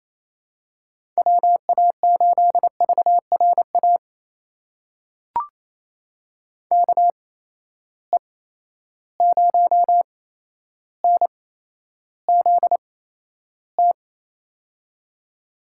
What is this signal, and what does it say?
Morse code, heard as a keyed tone, about 28 wpm, spaced as for 12 wpm, 700 Hz. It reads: WA8VRA KE0NZT
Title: USA Call Signs - 6 Characters - 8x Character Spacing - 28wpm